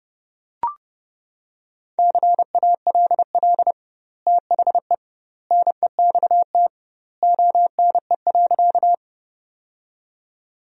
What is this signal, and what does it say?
Morse code, heard as a keyed tone, about 32 wpm, 700 Hz. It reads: CALL THE NEXT ONE.
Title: 2x Word-Spacing - Sentences from Top 300 Words - 30wpm